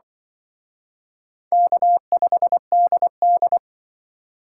Morse code, heard as a keyed tone, about 24 wpm, 700 Hz. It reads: K5DD
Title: Speed Racing - USA 1x2 Call Signs 20wpm